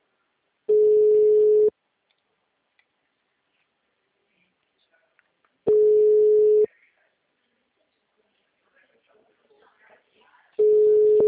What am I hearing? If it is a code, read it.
Ringback tone (Europe, 425 Hz), with speech.